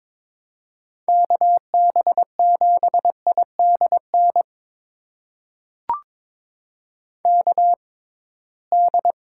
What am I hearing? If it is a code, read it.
Morse code, heard as a keyed tone, about 22 wpm, spaced as for 17 wpm, 700 Hz. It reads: KB7IDN KD
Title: USA Call Signs - 6x Character Spacing - 22wpm